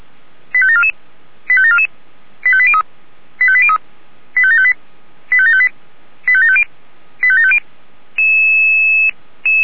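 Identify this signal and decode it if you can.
Five-tone selective call (ZVEI), heard as quick runs of five tones, tones of about 70 ms. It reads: 85744 85744 85803 85803 85758 85758 85750 85750
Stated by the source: ZVEI five-tone call 85729 received by SDR.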